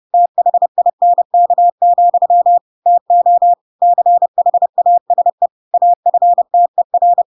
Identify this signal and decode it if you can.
Morse code, heard as a keyed tone, about 30 wpm, 700 Hz. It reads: THINK, TO CHASE AFTER
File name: Pinocchio / Pinocchio ch4 30 30 ik1hge